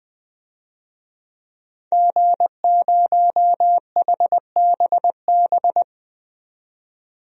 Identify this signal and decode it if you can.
Morse code, heard as a keyed tone, about 20 wpm, 700 Hz. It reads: G0HBB
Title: DXCC-100 Call Signs - 5 Characters - 20wpm